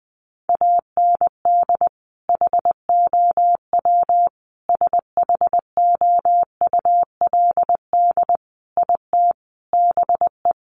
Morse code, heard as a keyed tone, about 20 wpm, 700 Hz. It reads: AND HOW SHOULD IT BE
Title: Sentences from Top 1200 Words 20wpm